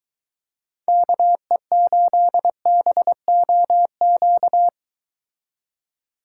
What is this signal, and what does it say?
Morse code, heard as a keyed tone, about 22 wpm, 700 Hz. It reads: KE8BOQ